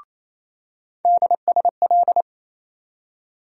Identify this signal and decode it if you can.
Morse code, heard as a keyed tone, about 28 wpm, 700 Hz. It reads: DSL
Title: USA Call Sign Postfixes - 3 Letters 28wpm